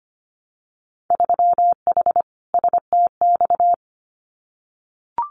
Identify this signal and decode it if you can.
Morse code, heard as a keyed tone, about 25 wpm, 700 Hz. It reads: 3H STX
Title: Winter Field Day Exchanges Encore 25wpm